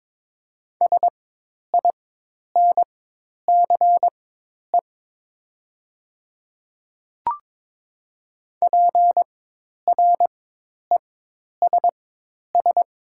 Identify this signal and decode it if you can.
Morse code, heard as a keyed tone, about 22 wpm, spaced as for 10 wpm, 700 Hz. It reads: SINCE PRESS
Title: 4x Character Spacing - Top 300 Words - 22wpm